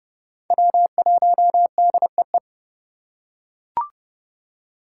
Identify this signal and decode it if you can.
Morse code, heard as a keyed tone, about 31 wpm, 700 Hz. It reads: W1DEE
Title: USA Call Signs - 5 Characters - 2x Character Spacing - 30wpm